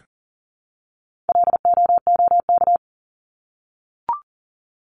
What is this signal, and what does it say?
Morse code, heard as a keyed tone, about 40 wpm, 700 Hz. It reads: LOOK
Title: ING Suffix - Top 100 Words with Base Words 40wpm